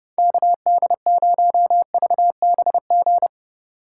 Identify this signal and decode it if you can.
Morse code, heard as a keyed tone, about 38 wpm, 700 Hz. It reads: KD0VBG